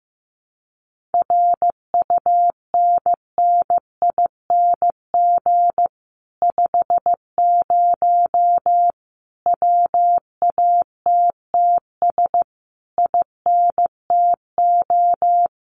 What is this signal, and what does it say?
Morse code, heard as a keyed tone, about 15 wpm, 700 Hz. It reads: RUNNING 50 WATTS INTO
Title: Common QSO Phrases Encore 15wpm